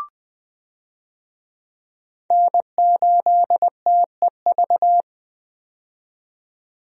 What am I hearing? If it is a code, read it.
Morse code, heard as a keyed tone, about 20 wpm, 700 Hz. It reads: N8TEV
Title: USA Call Signs - 5 Characters - 20wpm